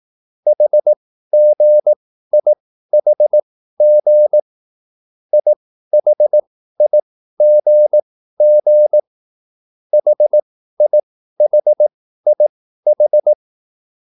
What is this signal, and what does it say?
Morse code, heard as a keyed tone, about 18 wpm, spaced as for 13 wpm, 600 Hz. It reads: HGIHG IHIGG HIHIH
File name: QSL / CW lesson